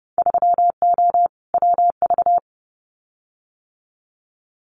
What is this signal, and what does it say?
Morse code, heard as a keyed tone, about 30 wpm, 700 Hz. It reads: 3O WV
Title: Winter Field Day Exchanges 30wpm